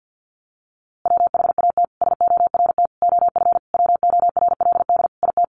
Morse code, heard as a keyed tone, about 50 wpm, 700 Hz. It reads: WHAT SORT OF WORLD IT
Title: A Sound Of Thunder 50wpm